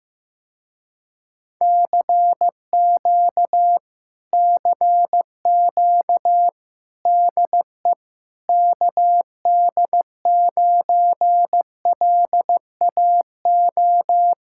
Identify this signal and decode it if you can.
Morse code, heard as a keyed tone, about 15 wpm, 700 Hz. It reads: CQ CQ DE KD9LAO
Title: Speed-Racing Common QSO Phrases Encore - 15wpm